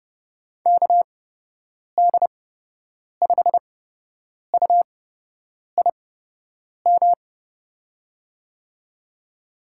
Morse code, heard as a keyed tone, about 29 wpm, spaced as for 8 wpm, 700 Hz. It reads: KD5UIM
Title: USA Call Signs - 6 Characters - 8x Character Spacing - 30wpm